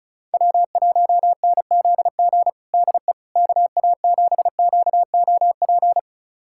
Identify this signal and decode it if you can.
Morse code, heard as a keyed tone, about 34 wpm, 700 Hz. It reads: W1NZG DE KA7QOP
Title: USA Callsign DE Callsign 35wpm